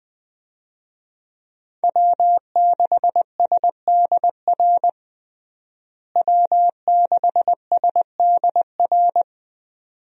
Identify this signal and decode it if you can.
Morse code, heard as a keyed tone, about 21 wpm, 700 Hz. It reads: W6SDR W6SDR
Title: USA Call Signs - 5 Characters - Repeated 2x - 20wpm